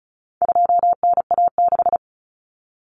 Morse code, heard as a keyed tone, about 34 wpm, 700 Hz. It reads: 2NA6
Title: Common Cut Numbers - 4-Digit 35wpm